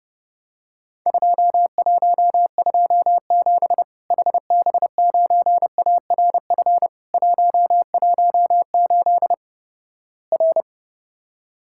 Morse code, heard as a keyed tone, about 30 wpm, 700 Hz. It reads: 2127 HB9ARF 118 R
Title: Worked All Europe DX Contest - QTC Exchange - 30wpm